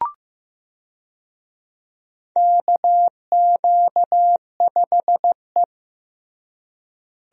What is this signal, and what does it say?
Morse code, heard as a keyed tone, about 15 wpm, 700 Hz. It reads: KQ5E
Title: USA Call Signs - 4 Characters - Repeated 3x - 15wpm